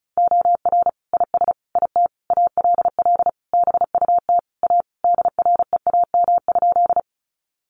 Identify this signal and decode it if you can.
Morse code, heard as a keyed tone, about 34 wpm, 700 Hz. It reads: OR IS IT ALL BUT A DREAM?
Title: Sentences from Top 1500 Words 35wpm